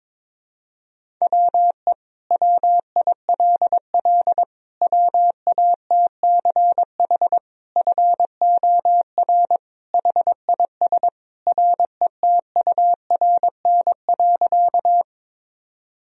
Morse code, heard as a keyed tone, about 22 wpm, 700 Hz. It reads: WE WILL WATCH FOR HIS RETURN.
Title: Sentences from Top 1000 Words - Review Part 1 - 22wpm